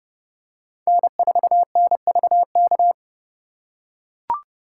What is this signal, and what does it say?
Morse code, heard as a keyed tone, about 29 wpm, 700 Hz. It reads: N4NVK